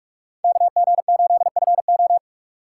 Morse code, heard as a keyed tone, about 41 wpm, 700 Hz. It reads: KG8FO
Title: USA Call Signs - 5 Characters - 4x Character Spacing - 45wpm